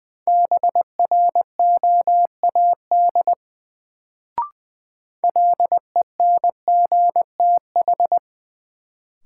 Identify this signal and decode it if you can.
Morse code, heard as a keyed tone, about 21 wpm, 700 Hz. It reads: BROAD LENGTH
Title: Sentences from Top 1000 Words - New Word Drill - 20wpm